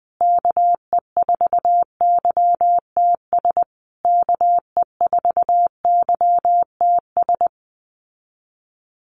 Morse code, heard as a keyed tone, about 20 wpm, 700 Hz. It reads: KE4YTS KE4YTS
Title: USA Call Signs Repeated 3x 20wpm